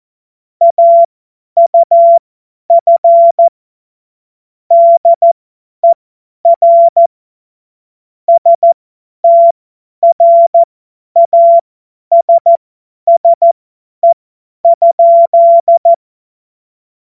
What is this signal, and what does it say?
Morse code, heard as a keyed tone, about 14 wpm, spaced as for 10 wpm, 680 Hz. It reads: AUF DER STRASSE?
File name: Morsekurs / Witze / Witz 16. August 2022 50 F70